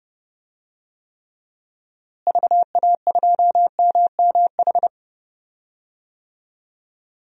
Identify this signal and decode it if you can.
Morse code, heard as a keyed tone, about 31 wpm, 700 Hz. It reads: VA2MMH
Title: DXCC-100 Call Signs - 6 Characters - Repeated 2x - 30wpm